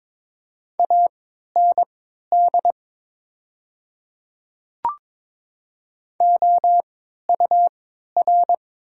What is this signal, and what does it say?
Morse code, heard as a keyed tone, about 22 wpm, spaced as for 13 wpm, 690 Hz. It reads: AND OUR
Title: 3x Character Spacing - Top 200 Words - 22wpm